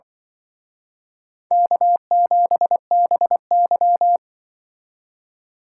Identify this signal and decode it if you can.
Morse code, heard as a keyed tone, about 25 wpm, 700 Hz. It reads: K7BY
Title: Speed Racing - USA 1x2 Call Signs 20wpm